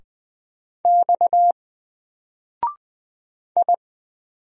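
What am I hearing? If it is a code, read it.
Morse code, heard as a keyed tone, about 20 wpm, 700 Hz. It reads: X I
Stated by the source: Single letter and number Morse drill at 20 wpm, from N4LCD.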